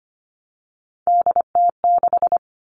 Morse code, heard as a keyed tone, about 25 wpm, 700 Hz. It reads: DT6
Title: All Cut Numbers - 3-Digit 25wpm